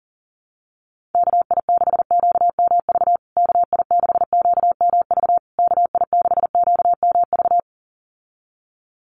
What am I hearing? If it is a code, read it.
Morse code, heard as a keyed tone, about 37 wpm, 700 Hz. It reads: KI6QMV KI6QMV KI6QMV
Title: USA Call Signs Repeated 3x 40wpm